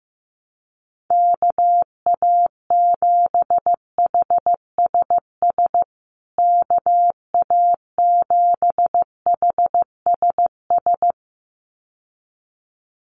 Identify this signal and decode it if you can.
Morse code, heard as a keyed tone, about 15 wpm, 700 Hz. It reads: KA7HSS KA7HSS
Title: USA Call Signs Repeated 2x 15wpm